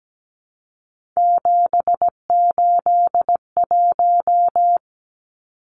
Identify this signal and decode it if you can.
Morse code, heard as a keyed tone, about 17 wpm, 700 Hz. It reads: 781
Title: Numbers - 3-Digit 17wpm